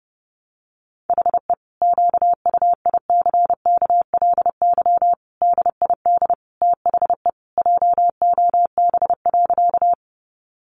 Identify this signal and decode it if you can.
Morse code, heard as a keyed tone, about 30 wpm, 700 Hz. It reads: HE QUICKLY DID THE JOB.
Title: Sentences from Top 700 Words 30wpm